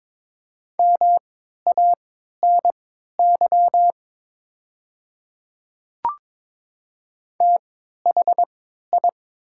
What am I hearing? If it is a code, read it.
Morse code, heard as a keyed tone, about 23 wpm, spaced as for 13 wpm, 700 Hz. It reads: MANY THI
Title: 3x Character Spacing - Top 100 Words - 22wpm